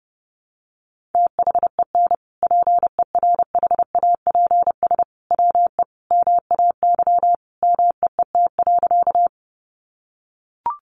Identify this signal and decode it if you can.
Morse code, heard as a keyed tone, about 30 wpm, 700 Hz. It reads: THEN PERHAPS WE MAY MEET.
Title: Sentences from Top 1100 Words 30wpm